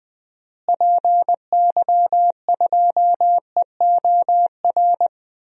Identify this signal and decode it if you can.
Morse code, heard as a keyed tone, about 20 wpm, 700 Hz. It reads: PY2EOR
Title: DXCC-100 Call Signs - 6 Characters - Repeated 2x - 20wpm